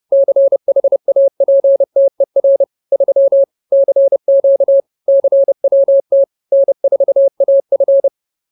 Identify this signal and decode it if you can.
Morse code, heard as a keyed tone, about 30 wpm, 550 Hz. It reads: CHAPTER 3 CQ CWT N4AF
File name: CWT-Chapter-03